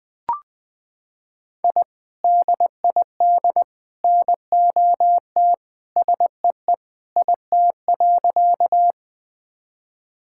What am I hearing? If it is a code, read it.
Morse code, heard as a keyed tone, about 21 wpm, 700 Hz. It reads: I DID NOT SEE IT.
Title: Sentences with 3-Word Phrases - 20wpm